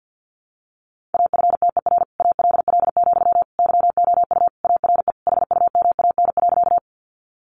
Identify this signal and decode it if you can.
Morse code, heard as a keyed tone, about 50 wpm, 700 Hz. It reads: AFTER ALL, YOU ARE HUMAN.
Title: Sentences from Top 600 Words 50wpm